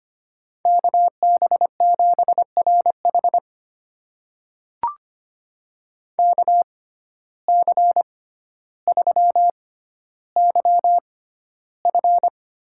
Morse code, heard as a keyed tone, about 25 wpm, spaced as for 13 wpm, 700 Hz. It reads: KB7RH KC3YF
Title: USA Call Signs - 5 Characters - 6x Character Spacing - 25wpm